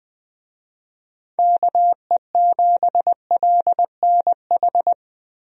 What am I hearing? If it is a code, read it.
Morse code, heard as a keyed tone, about 20 wpm, 700 Hz. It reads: KE7LNH
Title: USA Call Signs - 6 Characters - Repeated 3x - 20wpm